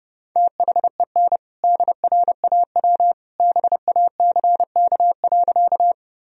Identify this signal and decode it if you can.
Morse code, heard as a keyed tone, about 30 wpm, 700 Hz. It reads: THEN DRAW BACK.
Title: Sentences from Top 1000 Words - Review Part 1 - 30wpm